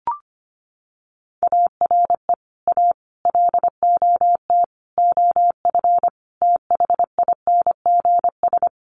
Morse code, heard as a keyed tone, about 25 wpm, 700 Hz. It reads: ARE A LOT OF THINGS
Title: Sets of 5 Words Encore2 25wpm